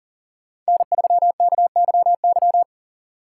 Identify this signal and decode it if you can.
Morse code, heard as a keyed tone, about 40 wpm, 700 Hz. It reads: N3KYY
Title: USA Call Signs - 5 Characters - 6x Character Spacing - 40wpm